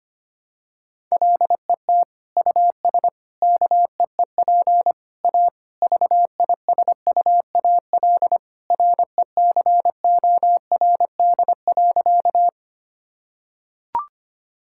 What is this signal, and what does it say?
Morse code, heard as a keyed tone, about 25 wpm, 700 Hz. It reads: LET US KEEP A VISUAL RECORD.
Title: Sentences from Top 1000 Words - Review Part 6 - 25wpm